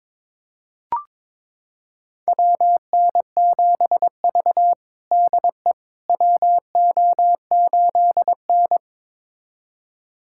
Morse code, heard as a keyed tone, about 22 wpm, 700 Hz. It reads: WN7V DE WO8N
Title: USA Callsign DE Callsign - 4 Characters - 22wpm